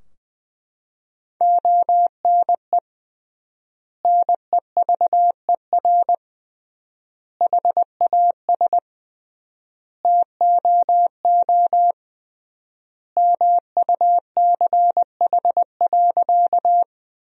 Morse code, heard as a keyed tone, about 21 wpm, 700 Hz. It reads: ONE NEVER HAS TOO MUCH.
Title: 3x Word-Spacing - Sentences from Top 300 Words - 20wpm